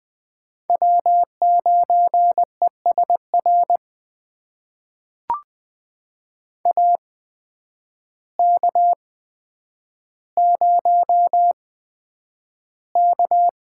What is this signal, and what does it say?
Morse code, heard as a keyed tone, about 20 wpm, spaced as for 9 wpm, 700 Hz. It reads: W9ESR AK0K
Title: USA Call Signs - 8x Character Spacing - 20wpm